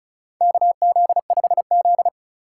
Morse code, heard as a keyed tone, about 36 wpm, 700 Hz. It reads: KZ5Z
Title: K1USN SST QSO Practice - 35wpm